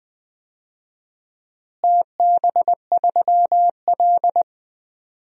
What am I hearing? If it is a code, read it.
Morse code, heard as a keyed tone, about 20 wpm, 700 Hz. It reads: TB3L
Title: DXCC-100 Call Signs - 4 Characters - Repeated 3x - 20wpm